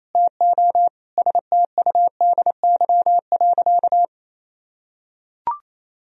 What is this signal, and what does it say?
Morse code, heard as a keyed tone, about 28 wpm, 700 Hz. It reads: TO STUDY.
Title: Sentences from Top 1000 Words - Review Part 1 - 28wpm